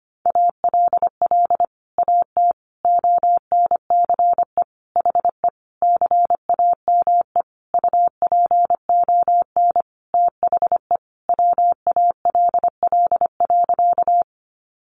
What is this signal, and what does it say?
Morse code, heard as a keyed tone, about 25 wpm, 700 Hz. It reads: ALL AT ONCE HE CAME UPON THE WALL.
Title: Sentences from Top 1100 Words 25wpm